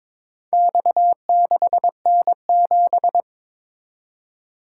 Morse code, heard as a keyed tone, about 22 wpm, 700 Hz. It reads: X6N7
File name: practice / 22wpm / Four Letter-Number Combinations - 22wpm